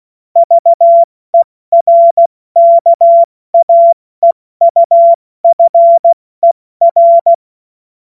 Morse code, heard as a keyed tone, about 16 wpm, 680 Hz. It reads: VERKAEUFER
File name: Morsekurs / Witze / Witz 20. Dezember 2022 70 F80